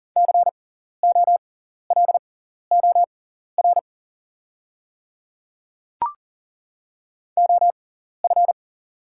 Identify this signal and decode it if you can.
Morse code, heard as a keyed tone, about 39 wpm, spaced as for 14 wpm, 690 Hz. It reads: COLOR OF
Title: 6x Character Spacing - Top 500 Words - 40wpm